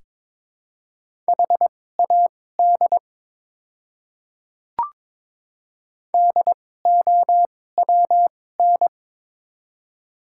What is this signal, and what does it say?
Morse code, heard as a keyed tone, about 22 wpm, spaced as for 16 wpm, 700 Hz. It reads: HAD DOWN
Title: 2x Character Spacing - Top 100 Words - 22wpm